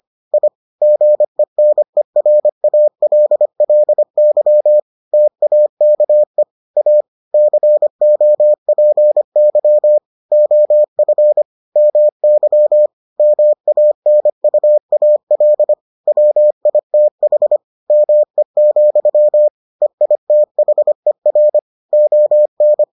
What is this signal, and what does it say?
Morse code, heard as a keyed tone, about 25 wpm, 600 Hz. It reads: I GENERALLY TAKE A COPY OF MY MANUAL WITH ME, EITHER ON